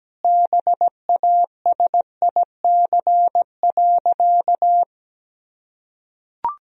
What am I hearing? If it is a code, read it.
Morse code, heard as a keyed tone, about 17 wpm, 700 Hz. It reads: BASIC.
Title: Sentences from Top 1000 Words - Review Part 3 - 17wpm